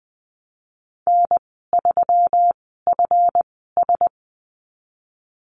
Morse code, heard as a keyed tone, about 20 wpm, spaced as for 15 wpm, 700 Hz. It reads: N3FS
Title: 179 Lesson 29g - Slash - Call Signs 20wpm